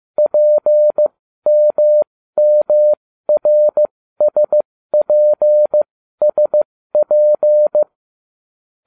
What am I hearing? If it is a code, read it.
Morse code, heard as a keyed tone, about 15 wpm, spaced as for 13 wpm, 600 Hz. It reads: PMMRSPSP